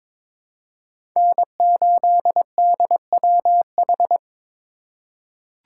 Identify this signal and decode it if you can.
Morse code, heard as a keyed tone, about 22 wpm, 700 Hz. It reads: N8DWH
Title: USA Call Signs - 5 Characters - 4x Character Spacing - 22wpm